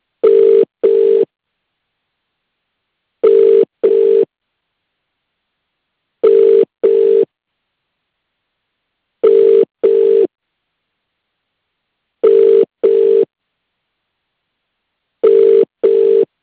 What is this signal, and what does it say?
Ringback tone (UK), 400 + 450 Hz.